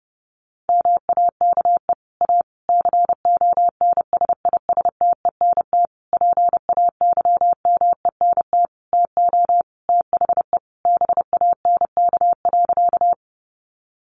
Morse code, heard as a keyed tone, about 28 wpm, 700 Hz. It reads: MAKE A CONSISTENT PAYMENT TO THE BANK.
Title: Sentences from Top 1500 Words 30wpm